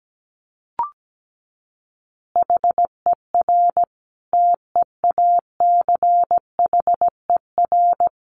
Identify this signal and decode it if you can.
Morse code, heard as a keyed tone, about 17 wpm, 700 Hz. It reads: HER TEACHER
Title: Additional Easy Phrases 17wpm